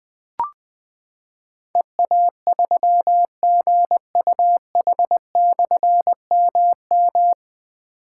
Morse code, heard as a keyed tone, about 20 wpm, 700 Hz. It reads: EA3GUH/MM
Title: Difficult Call Signs 20wpm